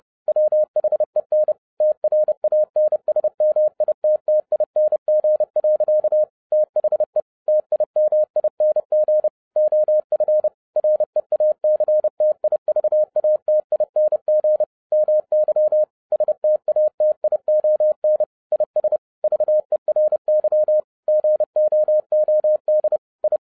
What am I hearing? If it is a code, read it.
Morse code, heard as a keyed tone, about 30 wpm, 610 Hz. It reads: WHEN TRANSMITTING. THE TIMING OF REACTIVATING MY STATION IS VERY GOOD I
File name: 2012-01-mp3 / n12-I DIDNT MISS THE SOLAR CYCLE 24-30wpm